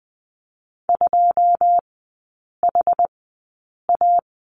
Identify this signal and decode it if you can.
Morse code, heard as a keyed tone, about 20 wpm, spaced as for 8 wpm, 700 Hz. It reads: 2HA
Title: 081 Lesson 14e - Number 2 - Three Characters 20wpm